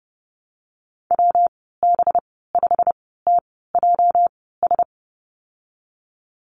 Morse code, heard as a keyed tone, about 32 wpm, spaced as for 17 wpm, 700 Hz. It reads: WB5TJS
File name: Lesson 37g - Letter x - Call Signs 30wpm